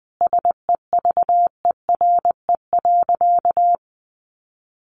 Morse code, heard as a keyed tone, about 20 wpm, 700 Hz. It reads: SEVERE.
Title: Sentences from Top 1400 Words 20wpm